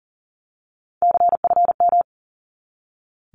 Morse code, heard as a keyed tone, about 40 wpm, 700 Hz. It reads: CFM
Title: Abbreviations 40wpm